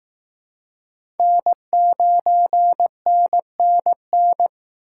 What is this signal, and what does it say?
Morse code, heard as a keyed tone, about 18 wpm, 700 Hz. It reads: N9NNN
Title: Speed Racing - USA 1x3 Call Signs 15wpm